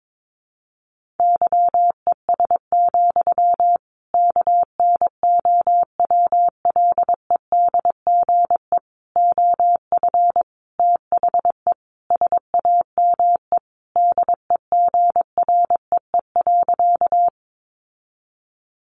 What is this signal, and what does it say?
Morse code, heard as a keyed tone, about 22 wpm, 700 Hz. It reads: YES, KNOWLEDGE OF THE SAME DEGREE.
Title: Sentences from Top 1500 Words 22wpm